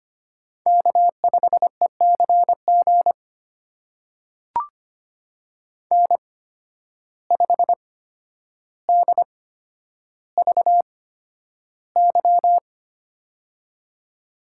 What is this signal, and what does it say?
Morse code, heard as a keyed tone, about 25 wpm, spaced as for 11 wpm, 700 Hz. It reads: K5ECG N5DVY
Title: USA Call Signs - 5 Characters - 8x Character Spacing - 25wpm